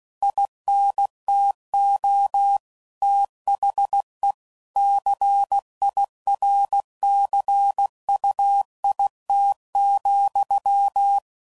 Morse code, heard as a keyed tone, about 16 wpm, 800 Hz. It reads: INTO THE CIRCUIT,
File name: CW Code Groups @15WPM / Technical Subjects / XC